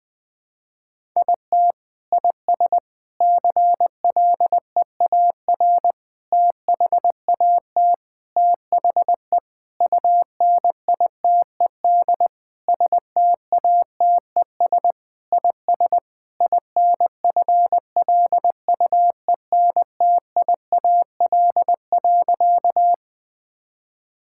Morse code, heard as a keyed tone, about 20 wpm, 700 Hz. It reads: IT IS CLEAR THAT THE UNITED STATES IS INFLUENTIAL.
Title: Sentences with 4-Word Phrases - 20wpm